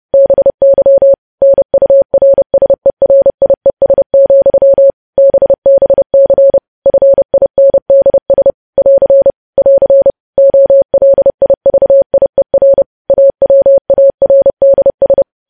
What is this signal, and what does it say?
Morse code, heard as a keyed tone, about 30 wpm, 550 Hz. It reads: BY NURSERIES, BBC FINDS <AR> <AR> OLIVIER AWARDS
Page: news headlines in morse code at 30 wpm